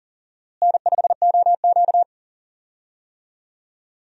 Morse code, heard as a keyed tone, about 37 wpm, 700 Hz. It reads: N5OQ
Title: USA Call Signs - 4 Characters - Repeated 2x - 40wpm